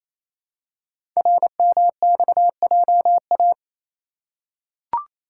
Morse code, heard as a keyed tone, about 28 wpm, 700 Hz. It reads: RMXJA
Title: Code Groups - 5 Letters - 28wpm